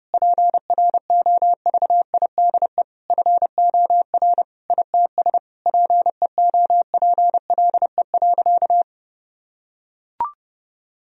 Morse code, heard as a keyed tone, about 30 wpm, 700 Hz. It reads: PROVIDE FOR ITS PEOPLE.
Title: Sentences from Top 1000 Words - Review Part 3 - 30wpm